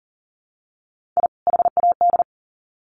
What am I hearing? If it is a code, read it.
Morse code, heard as a keyed tone, about 40 wpm, 700 Hz. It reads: I HAD
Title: Sets of 2 Words 40wpm